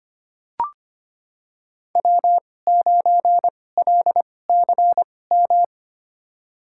Morse code, heard as a keyed tone, about 25 wpm, spaced as for 18 wpm, 700 Hz. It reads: W9LCM
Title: USA Call Signs - 5 Characters - 2x Character Spacing - 25wpm